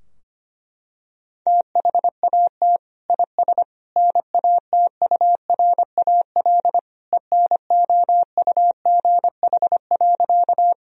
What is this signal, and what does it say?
Morse code, heard as a keyed tone, about 25 wpm, 700 Hz. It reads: THAT IS NATURAL ENOUGH.
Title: Sentences from Top 1000 Words - Review Part 4 - 25wpm